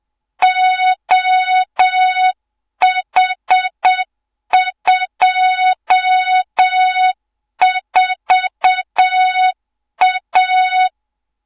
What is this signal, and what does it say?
Morse code, heard as a keyed tone, about 7 wpm, 3700 Hz, with noise. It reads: OH24A
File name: Morze-HA2MN / hang / 24 GYAKORLAT